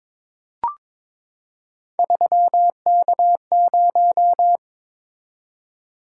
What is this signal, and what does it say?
Morse code, heard as a keyed tone, about 22 wpm, 700 Hz. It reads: 3K0
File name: practice / 22wpm / Three Letter-Number Combinations - 22wpm